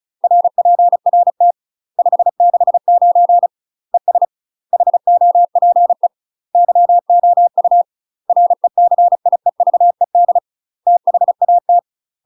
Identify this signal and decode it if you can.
Morse code, heard as a keyed tone, about 35 wpm, 700 Hz. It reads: RPRT 569 ES HOPE YOU RECIEVED THAT
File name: UK MASTER QSOs TRAINING / 35WPM / UK MASTER QSOs 0020